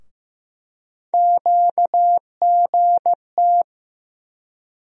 Morse code, heard as a keyed tone, about 15 wpm, 700 Hz. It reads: QGT
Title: USA Call Sign Postfixes - 3 Letters 15wpm